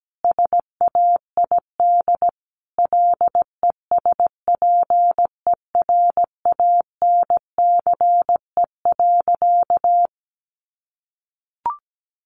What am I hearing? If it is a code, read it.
Morse code, heard as a keyed tone, about 17 wpm, 700 Hz. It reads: SAID LESPERANCE.
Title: A Sound Of Thunder 17wpm